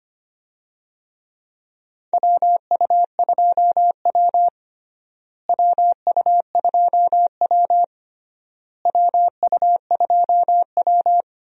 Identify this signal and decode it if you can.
Morse code, heard as a keyed tone, about 25 wpm, 700 Hz. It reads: WU2W WU2W WU2W
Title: USA Call Signs - 4 Characters - Repeated 3x - 25wpm